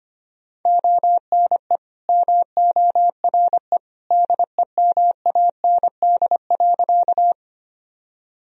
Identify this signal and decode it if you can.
Morse code, heard as a keyed tone, about 25 wpm, 700 Hz. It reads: ONE MORE DEMAND.
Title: Sentences from Top 1000 Words - Review Part 4 - 25wpm